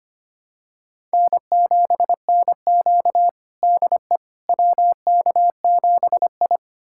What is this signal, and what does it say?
Morse code, heard as a keyed tone, about 25 wpm, 700 Hz. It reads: N7NQ DE WK7I
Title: USA Callsign DE Callsign - 4 Characters - 25wpm